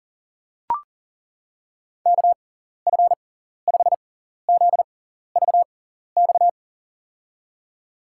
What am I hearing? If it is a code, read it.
Morse code, heard as a keyed tone, about 37 wpm, spaced as for 14 wpm, 700 Hz. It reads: KF5ZVX
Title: USA Call Signs - 6 Characters - 6x Character Spacing - 40wpm